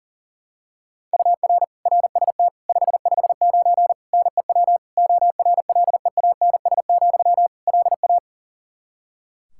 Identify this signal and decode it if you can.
Morse code, heard as a keyed tone, about 40 wpm, 700 Hz. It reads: UR RST 559 NEW ORLEANS, LA
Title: Practice QSO - Exchange 1 - Beginning - 40wpm